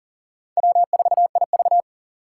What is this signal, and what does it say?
Morse code, heard as a keyed tone, about 51 wpm, 700 Hz. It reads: W4IV